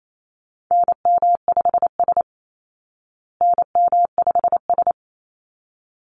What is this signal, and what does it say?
Morse code, heard as a keyed tone, about 28 wpm, 700 Hz. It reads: NM5S NM5S
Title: Call Signs Repeated 3x - 28wpm